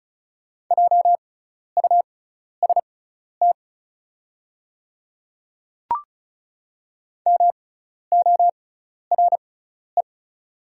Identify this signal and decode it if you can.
Morse code, heard as a keyed tone, about 36 wpm, spaced as for 12 wpm, 700 Hz. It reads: JUST MORE
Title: 6x Character Spacing - Top 200 Words - 35wpm